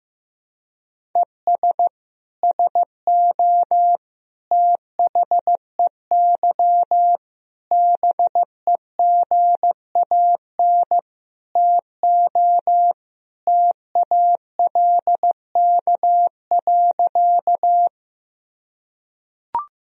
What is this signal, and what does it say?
Morse code, heard as a keyed tone, about 15 wpm, 700 Hz. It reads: ES SO THEY BEGAN TO TALK.
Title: Sentences from Top 400 Words and Abbreviations 15wpm